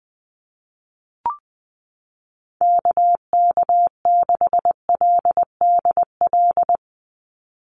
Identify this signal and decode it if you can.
Morse code, heard as a keyed tone, about 19 wpm, 700 Hz. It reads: KK6LDL
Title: USA Call Signs 20wpm